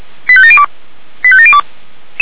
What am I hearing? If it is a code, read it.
Five-tone selective call (ZVEI), heard as quick runs of five tones, tones of about 70 ms. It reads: 85702 85702
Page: SDR/ZVEI